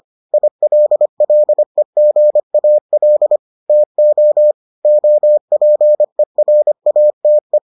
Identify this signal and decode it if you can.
Morse code, heard as a keyed tone, about 25 wpm, 600 Hz. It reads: ILLEGAL TO OPERATE